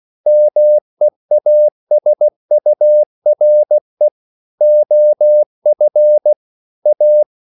Morse code, heard as a keyed tone, about 16 wpm, 600 Hz. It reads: MEASURE OF A
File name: CW / Practice / EnglishQuotations0216